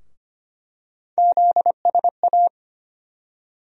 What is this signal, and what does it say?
Morse code, heard as a keyed tone, about 25 wpm, 700 Hz. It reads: ZSA
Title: USA Call Sign Postfixes - 3 Letters 25wpm